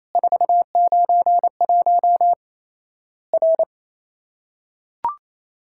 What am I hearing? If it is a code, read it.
Morse code, heard as a keyed tone, about 28 wpm, 700 Hz. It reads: 491 R
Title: Worked All Europe DX Contest - QTC Exchange - 28wpm